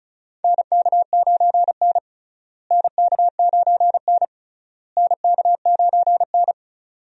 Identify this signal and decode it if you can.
Morse code, heard as a keyed tone, about 36 wpm, 700 Hz. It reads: NK9N NK9N NK9N
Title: USA Call Signs - 4 Characters - Repeated 3x - 35wpm